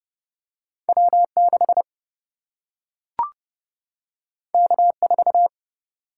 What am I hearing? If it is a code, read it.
Morse code, heard as a keyed tone, about 31 wpm, 700 Hz. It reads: W 6 K 4
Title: Single-Letter Single-Number 30wpm